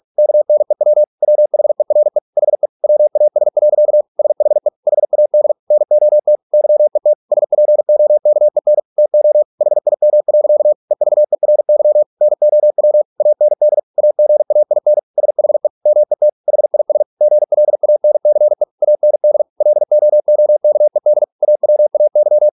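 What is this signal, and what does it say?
Morse code, heard as a keyed tone, about 46 wpm, 600 Hz. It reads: KNEW WHERE HE WAS. SHE HAD NOT YET SPOKEN TO HIM. EVERY NOW AND AGAIN SHE MET HIS GLANCE AND LOOKED AWAY
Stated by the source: Morse practice at 45 wpm, the novel Never by Ken Follett in English, from CW med Gnister.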